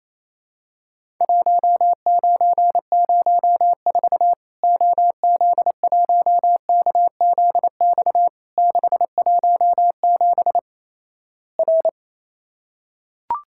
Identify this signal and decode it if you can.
Morse code, heard as a keyed tone, about 27 wpm, 700 Hz. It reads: 1904 OZ1KZX 617 R
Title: Worked All Europe DX Contest - QTC Exchange - 28wpm